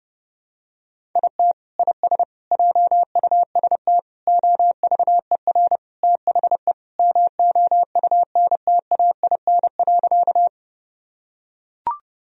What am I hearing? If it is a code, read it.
Morse code, heard as a keyed tone, about 30 wpm, 700 Hz. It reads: IT IS JUST OVER THE MOUNTAIN.
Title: Sentences from Top 1000 Words - Review Part 1 - 30wpm